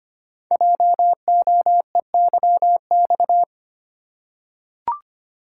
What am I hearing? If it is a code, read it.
Morse code, heard as a keyed tone, about 26 wpm, 700 Hz. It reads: JOEYX
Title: Code Groups - 5 Letter:Numbers - 25wpm